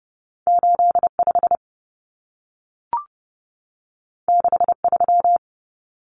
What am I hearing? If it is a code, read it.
Morse code, heard as a keyed tone, about 30 wpm, 700 Hz. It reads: 85 63
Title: Numbers - 2-Digit 30wpm